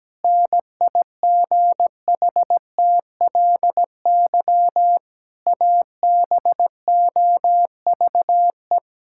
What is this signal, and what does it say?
Morse code, heard as a keyed tone, about 17 wpm, 700 Hz. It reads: NIGHTLY ABOVE
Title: Pangram Challenge v2 - 17wpm